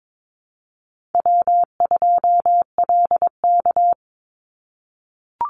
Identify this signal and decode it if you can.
Morse code, heard as a keyed tone, about 22 wpm, 700 Hz. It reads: W2LK
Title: Easy Call Signs - 22wpm